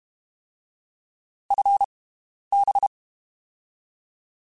Morse code, heard as a keyed tone, about 32 wpm, spaced as for 11 wpm, 800 Hz. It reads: FB
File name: (spaced) / QSOs Spaced-mp3s / WD0BGP 1 @30 sp